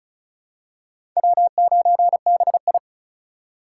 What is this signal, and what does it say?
Morse code, heard as a keyed tone, about 36 wpm, 700 Hz. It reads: W9BI
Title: USA Call Signs - 4 Characters - 2x Character Spacing - 35wpm